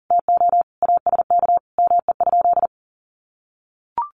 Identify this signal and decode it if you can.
Morse code, heard as a keyed tone, about 40 wpm, 700 Hz. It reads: TO ASK ME?
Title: Sentences from Top 1000 Words 40wpm